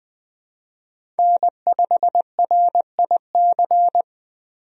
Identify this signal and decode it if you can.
Morse code, heard as a keyed tone, about 20 wpm, 700 Hz. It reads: N5RIC
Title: USA Call Signs - 5 Characters - Repeated 3x - 20wpm